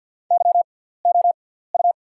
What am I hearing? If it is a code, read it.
Morse code, heard as a keyed tone, about 48 wpm, spaced as for 17 wpm, 700 Hz. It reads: YOU
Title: 6x Character Spacing - Top 100 Words - 50wpm